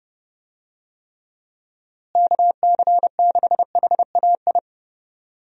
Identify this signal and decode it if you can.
Morse code, heard as a keyed tone, about 30 wpm, 700 Hz. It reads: KC6HAI
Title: USA Call Signs - 6 Characters - Repeated 3x - 30wpm